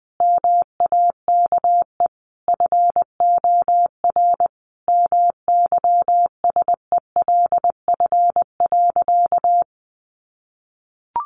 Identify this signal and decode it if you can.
Morse code, heard as a keyed tone, about 20 wpm, 700 Hz. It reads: MAKE FOR MYSELF.
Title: Sentences from Top 1300 Words 20wpm